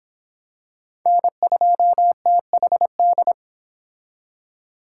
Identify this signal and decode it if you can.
Morse code, heard as a keyed tone, about 25 wpm, 700 Hz. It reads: N2THD